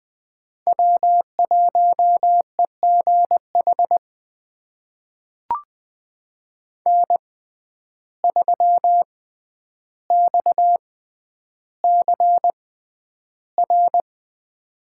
Morse code, heard as a keyed tone, about 20 wpm, spaced as for 10 wpm, 700 Hz. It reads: W1EGH N3XCR
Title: USA Call Signs - 5 Characters - 6x Character Spacing - 20wpm